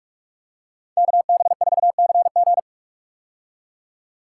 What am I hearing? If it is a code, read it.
Morse code, heard as a keyed tone, about 46 wpm, 700 Hz. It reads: KB4CG
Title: K1USN SST QSO Practice - 45wpm